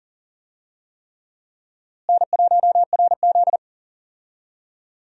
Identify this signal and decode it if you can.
Morse code, heard as a keyed tone, about 40 wpm, 700 Hz. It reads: N1RZ
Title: USA Call Signs - 4 Characters - Repeated 3x - 40wpm